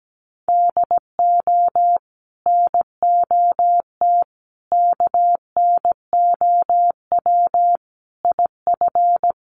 Morse code, heard as a keyed tone, about 17 wpm, 700 Hz. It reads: DO NOT KNOW IF
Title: Sets of 5 Words Encore2 17wpm